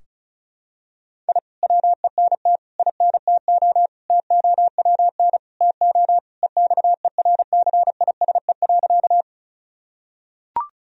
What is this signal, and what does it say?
Morse code, heard as a keyed tone, about 34 wpm, 700 Hz. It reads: I WENT INTO TOWN TO EXERCISE.
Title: Sentences from Top 1000 Words - Review Part 4 - 35wpm